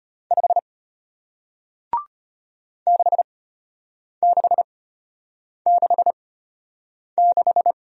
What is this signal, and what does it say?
Morse code, heard as a keyed tone, about 34 wpm, spaced as for 8 wpm, 700 Hz. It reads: 5 6666